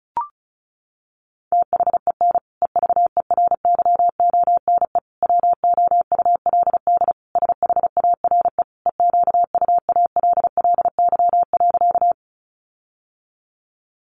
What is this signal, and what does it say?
Morse code, heard as a keyed tone, about 34 wpm, 700 Hz. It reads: THEN EVERYONE WOULD SHARE EQUALLY.
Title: Sentences from Top 1400 Words 35wpm